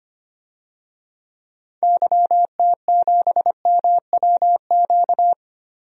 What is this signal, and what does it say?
Morse code, heard as a keyed tone, about 25 wpm, 700 Hz. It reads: YT7MWQ
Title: DXCC-100 Call Signs - 6 Characters - Repeated 3x - 25wpm